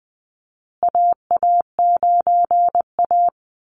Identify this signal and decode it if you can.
Morse code, heard as a keyed tone, about 20 wpm, 700 Hz. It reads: AA9A
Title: Call Signs Repeated 2x - 20wpm